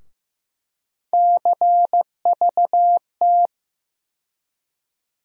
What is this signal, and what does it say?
Morse code, heard as a keyed tone, about 15 wpm, 700 Hz. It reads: CVT
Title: USA Call Sign Postfixes - 3 Letters 15wpm